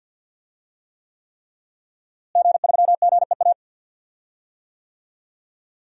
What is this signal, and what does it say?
Morse code, heard as a keyed tone, about 52 wpm, 700 Hz. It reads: G3GEA
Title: DXCC-100 Call Signs - 5 Characters - Repeated 3x - 50wpm